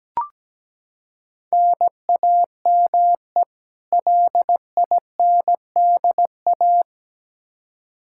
Morse code, heard as a keyed tone, about 17 wpm, 700 Hz. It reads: NAME LINDA
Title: Practice QSO - Exchange 1 - Beginning - 17wpm